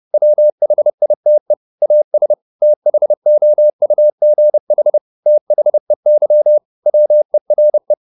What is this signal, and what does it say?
Morse code, heard as a keyed tone, about 30 wpm, 600 Hz. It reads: WHITE AS THOUGH THEY WERE
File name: the old man and the sea / the old man and the sea 1 30wpm